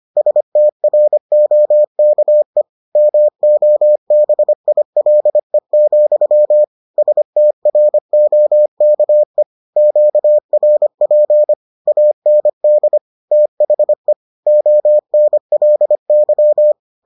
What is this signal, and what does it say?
Morse code, heard as a keyed tone, about 25 wpm, 600 Hz. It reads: STROKE MOBILE, STROKE QRP AND THE ONLY